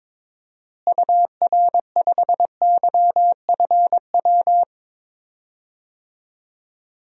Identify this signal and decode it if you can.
Morse code, heard as a keyed tone, about 22 wpm, 700 Hz. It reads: UR5YFW
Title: DXCC-100 Call Signs - 6 Characters - Repeated 2x - 22wpm